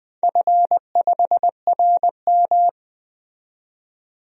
Morse code, heard as a keyed tone, about 21 wpm, 700 Hz. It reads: F5RM
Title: DXCC-100 Call Signs - 4 Characters - 20wpm